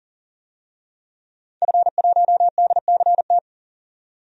Morse code, heard as a keyed tone, about 40 wpm, 700 Hz. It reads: F1DCT
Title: DXCC-100 Call Signs - 5 Characters - 40wpm